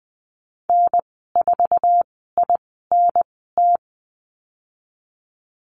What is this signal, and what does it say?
Morse code, heard as a keyed tone, about 19 wpm, spaced as for 14 wpm, 700 Hz. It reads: N4INT
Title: 088 Lesson 15g - Number 5 - Call Signs 20wpm